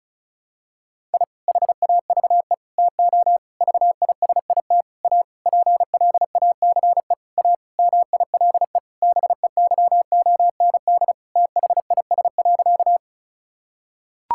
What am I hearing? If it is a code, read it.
Morse code, heard as a keyed tone, about 36 wpm, 700 Hz. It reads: I HAVE TO VISIT A PLACE A MILE BEYOND THIS.
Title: Sentences from Top 1000 Words - Review Part 6 - 35wpm